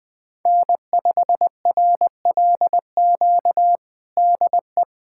Morse code, heard as a keyed tone, about 20 wpm, 700 Hz. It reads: N5RLQ DE
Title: USA Callsign DE Callsign - 5 Characters - 20wpm